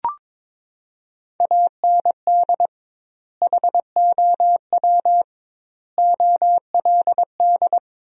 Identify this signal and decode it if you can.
Morse code, heard as a keyed tone, about 22 wpm, 700 Hz. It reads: AND HOW OLD
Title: 2x Word-Spacing - Sentences from Top 200 Words - 22wpm